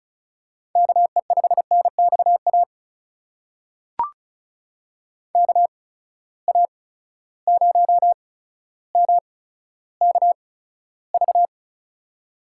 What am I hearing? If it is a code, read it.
Morse code, heard as a keyed tone, about 36 wpm, spaced as for 15 wpm, 700 Hz. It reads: KE5NXA KA0MKV
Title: USA Call Signs - 6 Characters - 8x Character Spacing - 35wpm